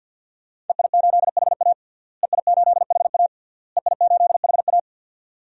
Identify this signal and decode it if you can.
Morse code, heard as a keyed tone, about 46 wpm, 700 Hz. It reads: EI8HA EI8HA EI8HA
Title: DXCC-100 Call Signs - 5 Characters - Repeated 3x - 50wpm